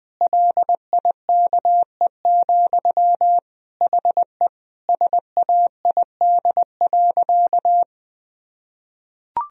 Morse code, heard as a keyed tone, about 20 wpm, 700 Hz. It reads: LIKE, HE SAID.
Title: Sentences from Top 1000 Words - Review Part 2 - 20wpm